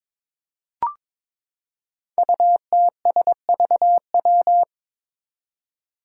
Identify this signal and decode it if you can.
Morse code, heard as a keyed tone, about 22 wpm, 700 Hz. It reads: UTSVW
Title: Five Challenging Letter-Number Combinations 22wpm